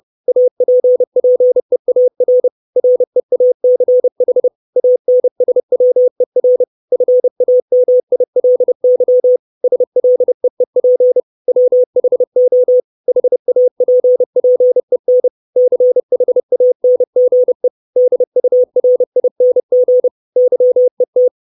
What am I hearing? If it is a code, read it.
Morse code, heard as a keyed tone, about 30 wpm, 500 Hz. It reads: APPEAR REACH ANSWER FAMILY SLEEP WHO HAPPEN CHANGE DURING YET
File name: 632 Words 30WPM